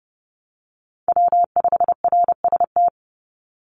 Morse code, heard as a keyed tone, about 30 wpm, 700 Hz. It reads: W5RST
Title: NA SOTA Chaser Callsigns 30wpm